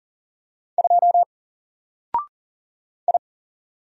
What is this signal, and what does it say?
Morse code, heard as a keyed tone, about 40 wpm, 700 Hz. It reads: 2 I
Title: Single Letter-Number 40wpm